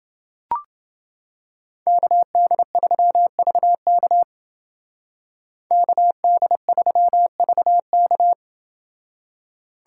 Morse code, heard as a keyed tone, about 29 wpm, 700 Hz. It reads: KD3VK KD3VK